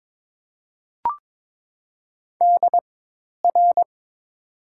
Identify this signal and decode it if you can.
Morse code, heard as a keyed tone, about 22 wpm, spaced as for 10 wpm, 700 Hz. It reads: DR
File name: Top 90 QSO Elements - 4x Character Spacing - 22wpm